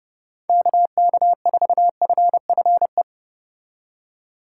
Morse code, heard as a keyed tone, about 30 wpm, 700 Hz. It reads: KK4FFE